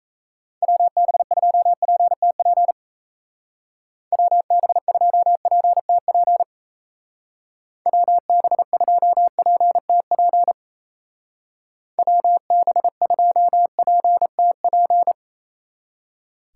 Morse code, heard as a keyed tone, about 34 wpm, 700 Hz. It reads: WB2PTP WB2PTP WB2PTP WB2PTP
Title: Speed Racing - USA 2x3 Call Signs 28wpm